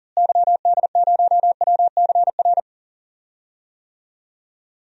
Morse code, heard as a keyed tone, about 40 wpm, 700 Hz. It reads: YD0WCR
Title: DXCC-100 Call Signs - 6 Characters - Repeated 3x - 40wpm